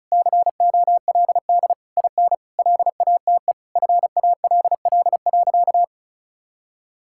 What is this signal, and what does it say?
Morse code, heard as a keyed tone, about 34 wpm, 700 Hz. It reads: COLD IN LATE FALL.
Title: Sentences from Top 1000 Words - Review Part 6 - 35wpm